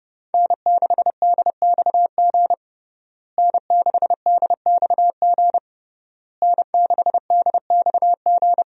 Morse code, heard as a keyed tone, about 30 wpm, 700 Hz. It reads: N6DXG N6DXG N6DXG
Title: USA Call Signs - 5 Characters - Repeated 3x - 30wpm